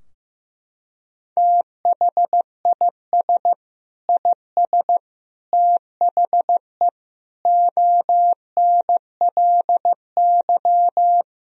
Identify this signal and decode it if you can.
Morse code, heard as a keyed tone, about 15 wpm, 700 Hz. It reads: THIS IS THE ONLY
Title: Sentences from Top 1000 Words - Review Part 3 - 15wpm